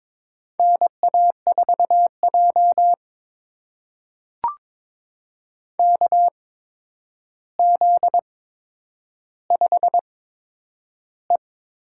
Morse code, heard as a keyed tone, about 22 wpm, spaced as for 9 wpm, 700 Hz. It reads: NA4J KZ5E
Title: USA Call Signs - 4 Characters - 8x Character Spacing - 22wpm